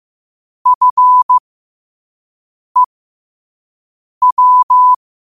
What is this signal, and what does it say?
Morse code, heard as a keyed tone, about 15 wpm, spaced as for 5 wpm, 1000 Hz. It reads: FEW
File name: qotdfiles / 5wpm